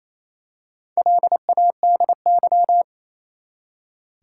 Morse code, heard as a keyed tone, about 28 wpm, 700 Hz. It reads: LADY
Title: Four Letter Words - 28wpm